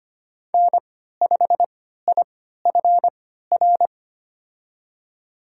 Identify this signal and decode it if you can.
Morse code, heard as a keyed tone, about 25 wpm, spaced as for 14 wpm, 700 Hz. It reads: N5IFR
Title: USA Call Signs - 5 Characters - 3x Character Spacing - 25wpm